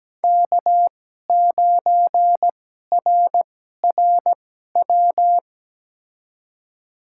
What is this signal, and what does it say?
Morse code, heard as a keyed tone, about 17 wpm, spaced as for 12 wpm, 700 Hz. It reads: K9RRW
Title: USA Call Signs - 2x Character Spacing - 17wpm